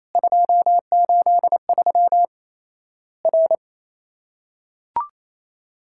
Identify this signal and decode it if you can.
Morse code, heard as a keyed tone, about 27 wpm, 700 Hz. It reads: 283 R
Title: Worked All Europe DX Contest - QTC Exchange - 28wpm